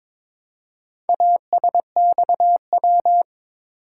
Morse code, heard as a keyed tone, about 22 wpm, 700 Hz. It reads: ASXW
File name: practice / 22wpm / Four Letter-Number Combinations - 22wpm